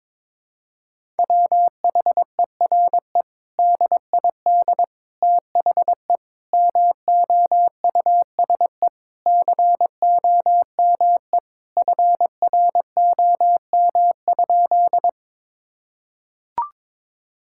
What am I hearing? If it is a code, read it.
Morse code, heard as a keyed tone, about 22 wpm, 700 Hz. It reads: WHERE DID THE MOUSE COME FROM?
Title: Sentences from Top 1000 Words - Review Part 5 - 22wpm